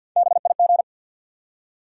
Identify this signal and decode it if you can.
Morse code, heard as a keyed tone, about 54 wpm, 700 Hz. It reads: BIG
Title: Three Letter Words - 50wpm